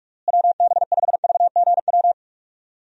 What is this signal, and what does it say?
Morse code, heard as a keyed tone, about 46 wpm, 700 Hz. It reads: WB5VGW
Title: USA Call Signs - 6 Characters - Repeated 3x - 45wpm